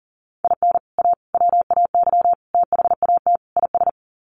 Morse code, heard as a keyed tone, about 40 wpm, 700 Hz. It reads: IN A WAY THAT IS
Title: Sets of 5 Words Encore 40wpm